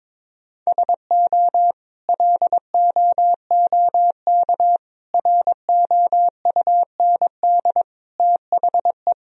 Morse code, heard as a keyed tone, about 21 wpm, 700 Hz. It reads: SO LOOK ROUND THE
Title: Sentences from Top 1000 Words - Review Part 1 - 22wpm